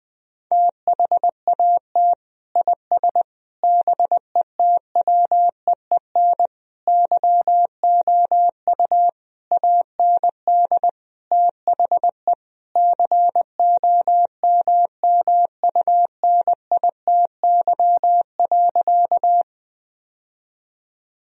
Morse code, heard as a keyed tone, about 20 wpm, 700 Hz. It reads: THAT IS BETWEEN YOU AND THE COMMUNITY.
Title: Sentences from Top 1000 Words - Review Part 4 - 20wpm